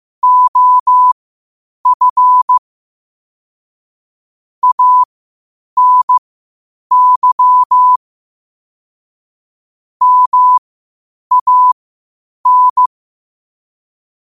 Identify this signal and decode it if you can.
Morse code, heard as a keyed tone, about 15 wpm, spaced as for 8 wpm, 1000 Hz. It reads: OF ANY MAN
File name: qotdfiles / 8wpm